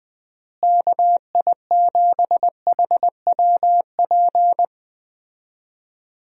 Morse code, heard as a keyed tone, about 20 wpm, 700 Hz. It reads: KI7HWP
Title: USA Call Signs - 6 Characters - Repeated 3x - 20wpm